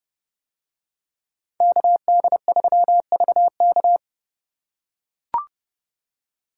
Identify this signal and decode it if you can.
Morse code, heard as a keyed tone, about 29 wpm, 700 Hz. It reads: KD3VK